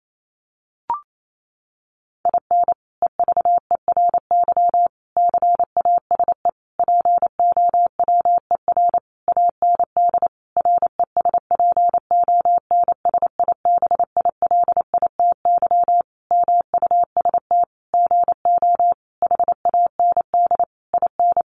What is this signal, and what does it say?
Morse code, heard as a keyed tone, about 28 wpm, 700 Hz. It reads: IN EVERY CASE POWER AND RESPONSIBILITY MUST GO HAND IN
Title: Sentences from Top 1200 Words 28wpm